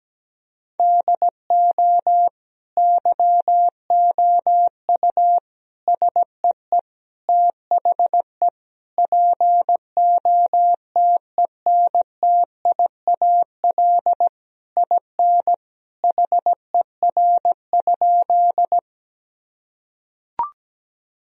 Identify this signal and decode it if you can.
Morse code, heard as a keyed tone, about 17 wpm, 700 Hz. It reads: DO YOU SEE THE POTENTIAL IN HER?
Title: Sentences from Top 1000 Words - Review Part 4 - 17wpm